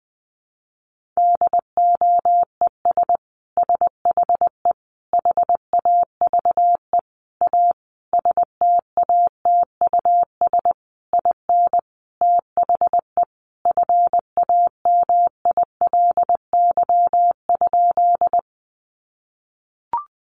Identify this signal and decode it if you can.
Morse code, heard as a keyed tone, about 20 wpm, 700 Hz. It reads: DOES SHE HAVE A STATUS IN THE FAMILY?
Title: Sentences from Top 1400 Words 20wpm